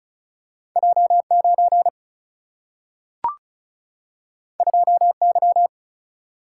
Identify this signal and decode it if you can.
Morse code, heard as a keyed tone, about 36 wpm, 700 Hz. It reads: J 9 2 Y
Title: Single-Letter Single-Number 35wpm